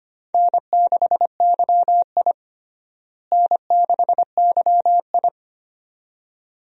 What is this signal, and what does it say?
Morse code, heard as a keyed tone, about 25 wpm, 700 Hz. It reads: N6YI N6YI
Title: USA Call Signs - 4 Characters - Repeated 2x - 25wpm